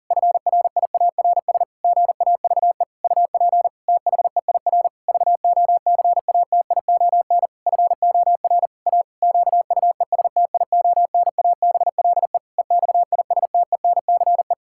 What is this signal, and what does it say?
Morse code, heard as a keyed tone, about 40 wpm, 700 Hz. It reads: FRIARS GAVE UP THEIR VOCATION FOR A QUESTIONABLE EXISTENCE
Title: Pangram Challenge v2 - 40wpm